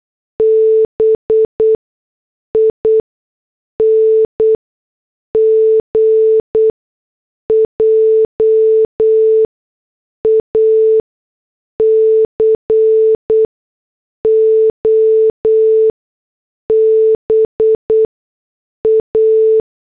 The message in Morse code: BING JACOBA